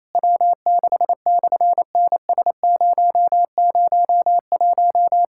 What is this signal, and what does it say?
Morse code, heard as a keyed tone, about 27 wpm, 700 Hz. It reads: W6/NS001
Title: SOTA QSO Practice 28wpm